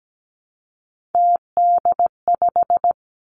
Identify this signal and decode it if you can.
Morse code, heard as a keyed tone, about 17 wpm, 700 Hz. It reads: TD5
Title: Three Challenging Letter-Number Combinations 17wpm